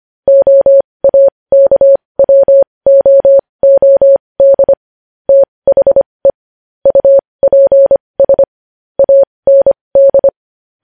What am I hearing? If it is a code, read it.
Morse code, heard as a keyed tone, about 25 wpm, spaced as for 20 wpm, 550 Hz. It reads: OAKWOOD THE UPS AND
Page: news-headlines-in-morse-code-at-20-wpm